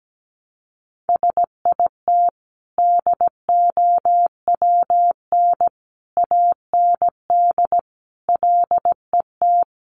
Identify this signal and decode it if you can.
Morse code, heard as a keyed tone, about 17 wpm, 700 Hz. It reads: SIT DOWN AND LET
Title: Sentences from Top 500 Words 17wpm